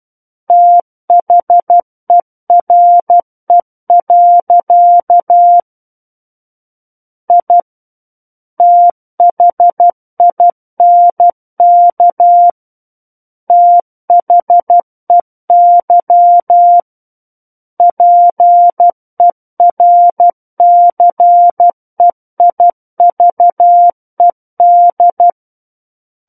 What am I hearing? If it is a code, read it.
Morse code, heard as a keyed tone, about 12 wpm, 700 Hz. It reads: THERE. I THINK THEY PERCEIVED
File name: Book One / 03-Chapter 3 - 12 WPM